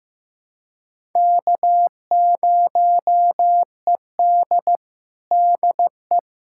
Morse code, heard as a keyed tone, about 15 wpm, 700 Hz. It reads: K0ED DE
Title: USA Callsign DE Callsign - 4 Characters - 15wpm